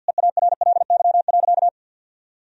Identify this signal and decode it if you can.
Morse code, heard as a keyed tone, about 49 wpm, 700 Hz. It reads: EARLY.
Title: Sentences from Top 1000 Words - Review Part 2 - 50wpm